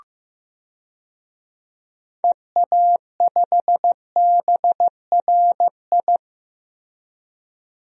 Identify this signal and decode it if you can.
Morse code, heard as a keyed tone, about 15 wpm, 700 Hz. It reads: EA5BRI
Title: DXCC-100 Call Signs - 6 Characters - Repeated 3x - 15wpm